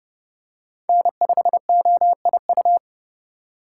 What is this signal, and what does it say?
Morse code, heard as a keyed tone, about 28 wpm, 700 Hz. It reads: N5OIU